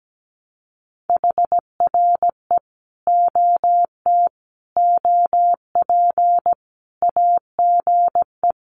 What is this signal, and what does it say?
Morse code, heard as a keyed tone, about 17 wpm, 700 Hz. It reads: HRE OT OP AGE
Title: Common QSO Phrases 17wpm